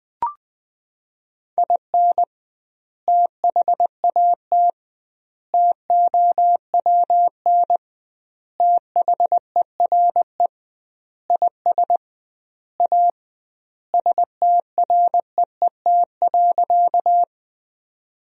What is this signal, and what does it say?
Morse code, heard as a keyed tone, about 20 wpm, 700 Hz. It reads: IN THAT TOWN THERE IS A STREET.
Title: 2x Word-Spacing - Sentences from Top 500 Words - 20wpm